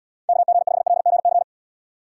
Morse code, heard as a keyed tone, about 78 wpm, 700 Hz. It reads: KG5LGC